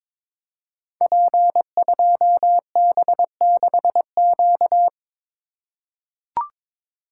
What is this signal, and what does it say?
Morse code, heard as a keyed tone, about 22 wpm, 700 Hz. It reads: P2B6Q
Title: Code Groups - 5 Letter:Numbers - 22wpm